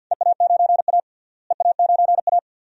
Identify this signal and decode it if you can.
Morse code, heard as a keyed tone, about 52 wpm, 700 Hz. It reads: EA9A EA9A
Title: DXCC-100 Call Signs - 4 Characters - Repeated 2x - 50wpm